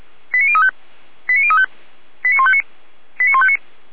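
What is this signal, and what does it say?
Five-tone selective call (ZVEI), heard as quick runs of five tones, tones of about 70 ms. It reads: 89035 89035 89159 89159
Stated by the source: ZVEI five-tone selective call received on SDR.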